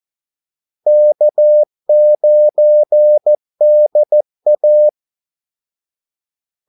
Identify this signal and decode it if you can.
Morse code, heard as a keyed tone, about 14 wpm, 600 Hz. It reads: K9DA
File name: Callsigns-14wpm